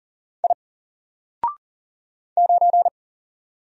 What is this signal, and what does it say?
Morse code, heard as a keyed tone, about 39 wpm, 700 Hz. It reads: I 9
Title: Single Character (Letter:Number:Punctuation:Prosign) 40wpm